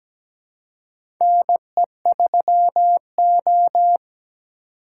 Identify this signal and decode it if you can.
Morse code, heard as a keyed tone, about 17 wpm, 700 Hz. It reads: NE3O
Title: USA Call Signs - 4 Characters - 2x Character Spacing - 17wpm